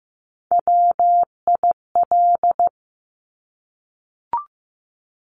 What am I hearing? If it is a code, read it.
Morse code, heard as a keyed tone, about 15 wpm, 700 Hz. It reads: WIL
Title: Top Three Letter Combinations in Words 15wpm